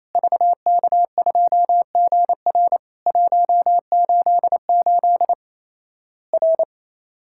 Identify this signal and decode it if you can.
Morse code, heard as a keyed tone, about 28 wpm, 700 Hz. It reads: VK2GR 188 R
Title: Worked All Europe DX Contest - QTC Exchange - 28wpm